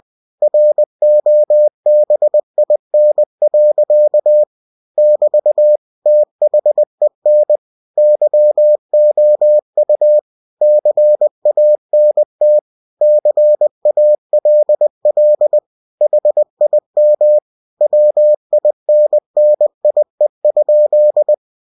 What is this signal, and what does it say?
Morse code, heard as a keyed tone, about 20 wpm, 600 Hz. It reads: ROBIN. <BT> THEN YOU CANT CALL HIM WINNIE?
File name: cw-books / winnie-the-pooh / winnie-20wpm-page-0000